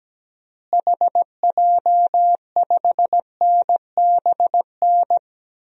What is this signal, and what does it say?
Morse code, heard as a keyed tone, about 17 wpm, 700 Hz. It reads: HJ5NBN
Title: DXCC-100 Call Signs - 6 Characters - Repeated 2x - 17wpm